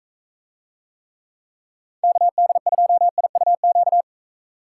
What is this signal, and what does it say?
Morse code, heard as a keyed tone, about 45 wpm, 700 Hz. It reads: KD2IUQ